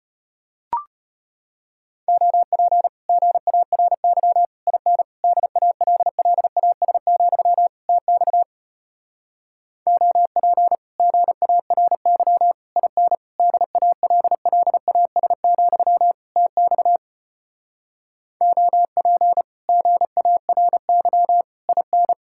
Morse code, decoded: OP GARY IN DALLAS, TX OP GARY IN DALLAS, TX OP GARY IN

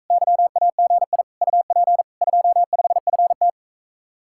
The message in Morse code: YAGI UP 25FT

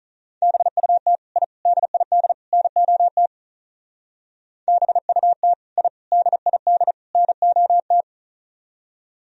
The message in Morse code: BUT I DID NOT BUT I DID NOT

